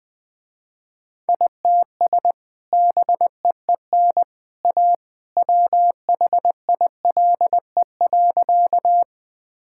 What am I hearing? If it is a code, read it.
Morse code, heard as a keyed tone, about 21 wpm, 700 Hz. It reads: ITS BEEN A WHILE.
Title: Sentences with 2-Word Phrases - Spoken Beginning - 20wpm